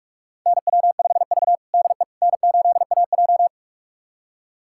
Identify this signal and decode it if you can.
Morse code, heard as a keyed tone, about 46 wpm, 700 Hz. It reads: NW5V DE N8AJ
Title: USA Callsign DE Callsign - 4 Characters - 45wpm